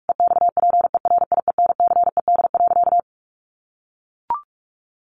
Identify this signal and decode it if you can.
Morse code, heard as a keyed tone, about 44 wpm, 700 Hz. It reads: EXPERIENCED.
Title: Sentences from Top 1100 Words 45wpm